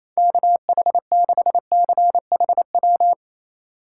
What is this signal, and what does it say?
Morse code, heard as a keyed tone, about 28 wpm, 700 Hz. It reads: KH6CHW
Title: USA Call Signs - 6 Characters - 3x Character Spacing - 28wpm